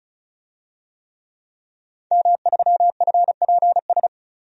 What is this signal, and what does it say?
Morse code, heard as a keyed tone, about 36 wpm, 700 Hz. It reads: M3FPS
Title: DXCC-100 Call Signs - 5 Characters - 35wpm